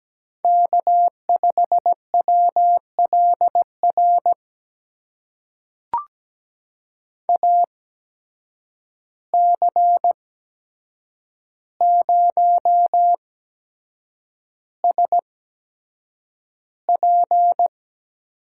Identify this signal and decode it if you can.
Morse code, heard as a keyed tone, about 17 wpm, spaced as for 7 wpm, 700 Hz. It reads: K5WLR AC0SP
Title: USA Call Signs - 5 Characters - 8x Character Spacing - 17wpm